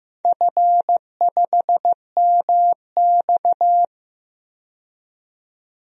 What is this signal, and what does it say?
Morse code, heard as a keyed tone, about 15 wpm, 700 Hz. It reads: F5MX
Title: DXCC-100 Call Signs - 4 Characters - 15wpm